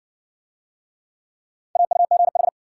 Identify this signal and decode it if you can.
Morse code, heard as a keyed tone, about 61 wpm, 700 Hz. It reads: AUGH